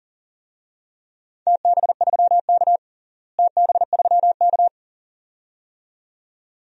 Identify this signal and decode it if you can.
Morse code, heard as a keyed tone, about 40 wpm, 700 Hz. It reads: TB3K TB3K
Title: DXCC-100 Call Signs - 4 Characters - Repeated 2x - 40wpm